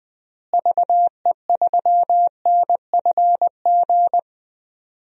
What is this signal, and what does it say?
Morse code, heard as a keyed tone, about 20 wpm, 700 Hz. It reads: VE3NFG
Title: DXCC-100 Call Signs - 6 Characters - Repeated 2x - 20wpm